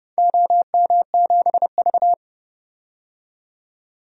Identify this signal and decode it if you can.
Morse code, heard as a keyed tone, about 30 wpm, 700 Hz. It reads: OM7V
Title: DXCC-100 Call Signs - 4 Characters - 30wpm